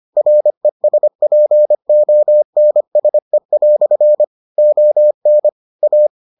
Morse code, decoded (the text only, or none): RESPONSE" ON A